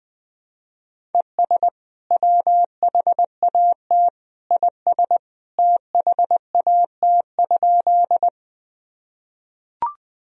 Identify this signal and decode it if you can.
Morse code, heard as a keyed tone, about 20 wpm, 700 Hz. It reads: ES WHAT IS THAT?
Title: Sentences from Top 100 Words and Abbreviations 20wpm